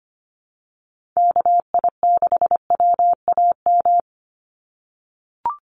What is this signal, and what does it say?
Morse code, heard as a keyed tone, about 25 wpm, 700 Hz. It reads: KI6WAM
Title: USA Call Signs 25wpm